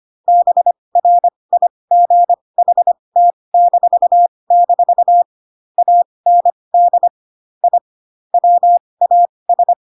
Morse code, heard as a keyed tone, about 25 wpm, spaced as for 20 wpm, 700 Hz. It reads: BRIGHT-- AND I WAS
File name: Pinocchio / Pinocchio ch7 15 25 ik1hge